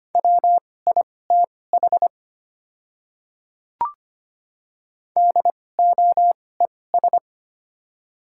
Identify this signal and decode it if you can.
Morse code, heard as a keyed tone, about 25 wpm, spaced as for 18 wpm, 700 Hz. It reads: WITH DOES
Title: 2x Character Spacing - Top 200 Words - 25wpm